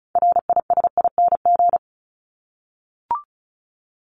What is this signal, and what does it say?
Morse code, heard as a keyed tone, about 34 wpm, 700 Hz. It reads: RISING
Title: ING Suffix - Top 100 Words with Base Words 35wpm